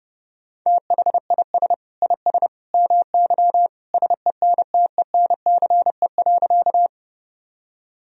Morse code, heard as a keyed tone, about 30 wpm, 700 Hz. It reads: THIS IS MY SENTENCE.
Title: Sentences from Top 1000 Words - Review Part 1 - 30wpm